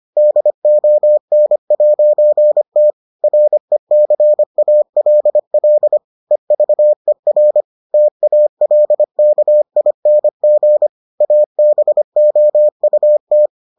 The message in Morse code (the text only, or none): DON'T RECALL EVER TALKING ABOUT